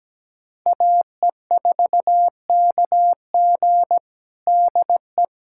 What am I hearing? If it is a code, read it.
Morse code, heard as a keyed tone, about 17 wpm, 700 Hz. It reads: AE4KG DE
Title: USA Callsign DE Callsign - 5 Characters - 17wpm